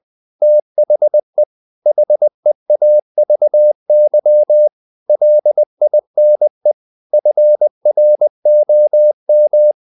Morse code, heard as a keyed tone, about 20 wpm, 600 Hz. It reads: THE HEAVY LINE FROM
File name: the old man and the sea part 2 20wpm